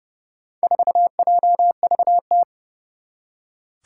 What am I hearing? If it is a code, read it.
Morse code, heard as a keyed tone, about 30 wpm, 700 Hz. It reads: 4JVT
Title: Four Challenging Letter-Number Combinations 30wpm